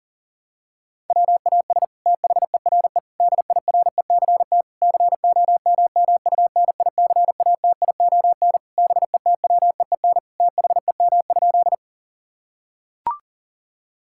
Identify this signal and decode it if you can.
Morse code, heard as a keyed tone, about 40 wpm, 700 Hz. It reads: WAS THERE DIRECT COMMUNICATION BETWEEN THEM?
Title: Sentences from Top 1000 Words - Review Part 6 - 40wpm